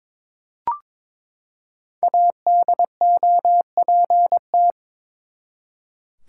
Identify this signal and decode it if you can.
Morse code, heard as a keyed tone, about 22 wpm, 700 Hz. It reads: ADOPT
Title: Five Letter Words - 22wpm